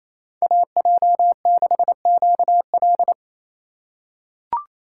Morse code, heard as a keyed tone, about 27 wpm, 700 Hz. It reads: AJ6QL
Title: K1USN SST QSO Practice - 28wpm